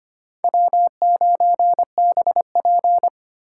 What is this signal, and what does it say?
Morse code, heard as a keyed tone, about 31 wpm, 700 Hz. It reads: W9BP
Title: Speed Racing - USA 1x2 Call Signs 25wpm